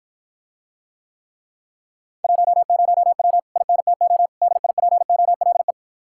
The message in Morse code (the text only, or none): 10W INTO DIPOLE